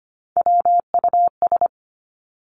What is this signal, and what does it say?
Morse code, heard as a keyed tone, about 25 wpm, 700 Hz. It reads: WUS
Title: Three Challenging Letter-Number Combinations 25wpm